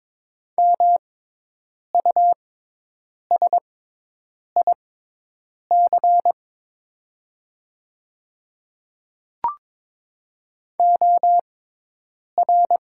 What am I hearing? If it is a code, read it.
Morse code, heard as a keyed tone, about 22 wpm, spaced as for 8 wpm, 700 Hz. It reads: MUSIC OR
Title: 6x Character Spacing - Top 500 Words - 22wpm